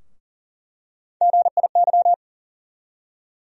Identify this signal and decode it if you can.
Morse code, heard as a keyed tone, about 40 wpm, 700 Hz. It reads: GIY
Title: USA Call Sign Postfixes - 3 Letters 40wpm